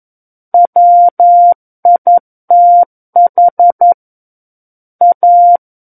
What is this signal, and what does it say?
Morse code, heard as a keyed tone, about 11 wpm, 700 Hz. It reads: WITH A
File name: Book One / 02-Chapter 2 - 11 WPM